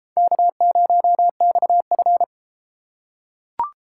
Morse code, heard as a keyed tone, about 31 wpm, 700 Hz. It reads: K0XF